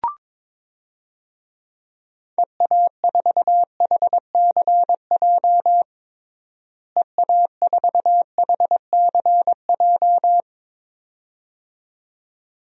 Morse code, heard as a keyed tone, about 22 wpm, 700 Hz. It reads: EA4HCJ EA4HCJ
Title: DXCC-100 Call Signs - 6 Characters - Repeated 2x - 22wpm